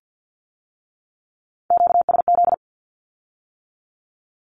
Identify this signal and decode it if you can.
Morse code, heard as a keyed tone, about 50 wpm, 700 Hz. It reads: QSZ
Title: Q-Codes with Spoken Meaning - 50wpm